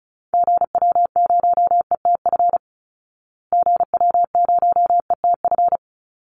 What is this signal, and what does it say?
Morse code, heard as a keyed tone, about 34 wpm, 700 Hz. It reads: GW0ETF GW0ETF
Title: Call Signs Repeated 2x - 35wpm